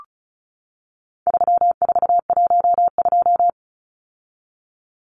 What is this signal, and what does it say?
Morse code, heard as a keyed tone, about 34 wpm, 700 Hz. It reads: 3412
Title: Numbers - 4-Digit 35wpm